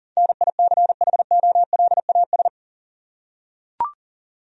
Morse code, heard as a keyed tone, about 40 wpm, 700 Hz. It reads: NICHOLAS
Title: USA Common Names - 40wpm